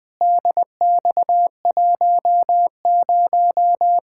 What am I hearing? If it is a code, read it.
Morse code, heard as a keyed tone, about 20 wpm, 700 Hz. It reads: DX10
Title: Practice QSO - Exchange 2 - Info - 20wpm